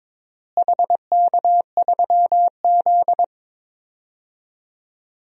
Morse code, heard as a keyed tone, about 22 wpm, 700 Hz. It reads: HK3Z
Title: DXCC-100 Call Signs - 4 Characters - Repeated 3x - 22wpm